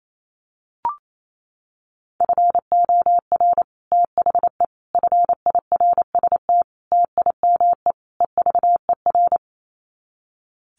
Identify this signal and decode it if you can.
Morse code, heard as a keyed tone, about 28 wpm, 700 Hz. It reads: FOR THE FIRST TIME EVER
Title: Sets of 5 Words Encore2 28wpm